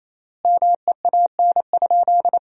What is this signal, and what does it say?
Morse code, heard as a keyed tone, about 28 wpm, 700 Hz. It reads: MEAN?
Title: 3x Word-Spacing - Sentences from Top 200 Words - 28wpm